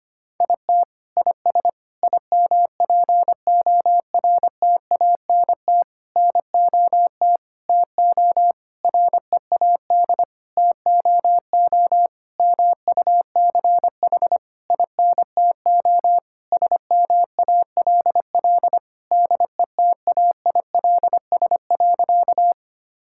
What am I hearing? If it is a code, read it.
Morse code, heard as a keyed tone, about 25 wpm, 700 Hz. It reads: IT IS IMPORTANT NOT TO READ TOO MUCH INTO SMALL DETAILS.
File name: practice / 25wpm / Sentences with 9-Word Phrases - 25wpm